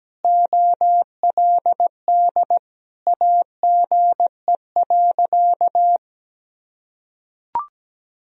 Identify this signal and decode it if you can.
Morse code, heard as a keyed tone, about 17 wpm, 700 Hz. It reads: OLD AGE.
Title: Sentences from Top 1000 Words - Review Part 3 - 17wpm